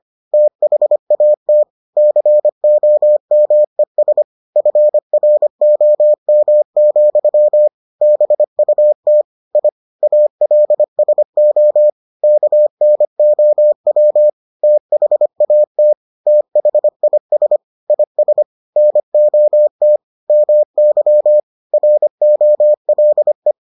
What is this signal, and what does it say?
Morse code, heard as a keyed tone, about 25 wpm, 600 Hz. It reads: THAT COMES FROM, BUT I ALSO KNOW THAT THIS IS NOT MY ROLE